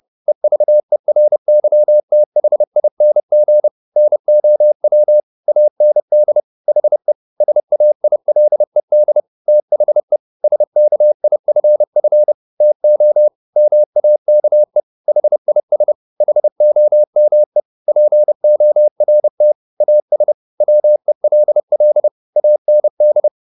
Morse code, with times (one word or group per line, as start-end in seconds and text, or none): 0.3-3.7 EVERYTHING
4.0-5.2 NOW
5.5-6.4 AND
6.7-7.1 HE
7.4-9.2 SAILED
9.5-10.2 THE
10.4-12.3 SKIFF
12.6-13.3 TO
13.6-14.8 MAKE
15.1-15.9 HIS
16.2-17.6 HOME
17.9-19.5 PORT
19.8-20.3 AS
20.6-22.1 WELL
22.4-23.3 AND